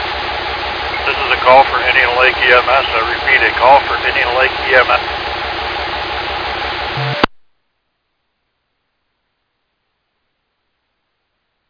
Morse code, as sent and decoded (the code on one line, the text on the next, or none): none